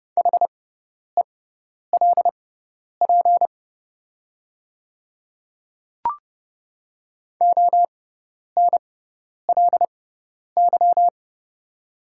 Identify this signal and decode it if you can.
Morse code, heard as a keyed tone, about 30 wpm, spaced as for 10 wpm, 700 Hz. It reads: HELP ONLY
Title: 6x Character Spacing - Top 300 Words - 30wpm